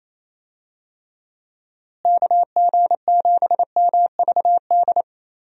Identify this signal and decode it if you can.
Morse code, heard as a keyed tone, about 28 wpm, 700 Hz. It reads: KG7MVD
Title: USA Call Signs - 6 Characters - Repeated 2x - 28wpm